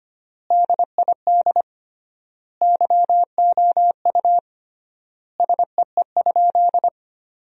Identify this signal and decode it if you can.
Morse code, heard as a keyed tone, about 25 wpm, 700 Hz. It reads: DID YOU SEE?
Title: 3x Word-Spacing - Sentences from Top 100 Words - 25wpm